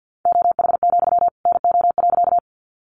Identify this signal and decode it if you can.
Morse code, heard as a keyed tone, about 50 wpm, 700 Hz. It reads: OH, NO.
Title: Sentences from Top 500 Words 50wpm